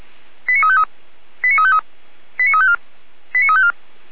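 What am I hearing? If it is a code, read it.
Five-tone selective call (ZVEI), heard as quick runs of five tones, tones of about 70 ms. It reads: 89353 89353 89354 89354
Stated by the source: ZVEI five-tone call 89503 received via SDR.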